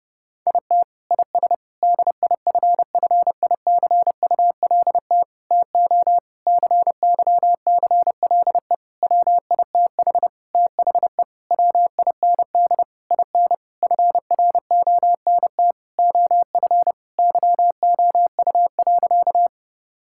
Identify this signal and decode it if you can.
Morse code, heard as a keyed tone, about 30 wpm, 700 Hz. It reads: IT IS DIFFICULT TO CYCLE WITH THE WIND IN FRONT OF YOU.
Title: Sentences from Top 1000 Words - Review Part 6 - 30wpm